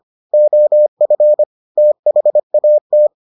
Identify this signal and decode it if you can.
Morse code, heard as a keyed tone, about 25 wpm, 600 Hz. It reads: OF THAT